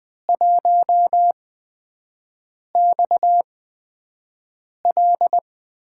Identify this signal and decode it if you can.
Morse code, heard as a keyed tone, about 20 wpm, spaced as for 5 wpm, 700 Hz. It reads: 1XL
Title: USA Call Signs - 4 Characters - 8x Character Spacing - 20wpm